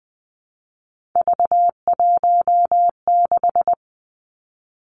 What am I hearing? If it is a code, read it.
Morse code, heard as a keyed tone, about 20 wpm, 700 Hz. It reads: V16
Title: Three Challenging Letter-Number Combinations 20wpm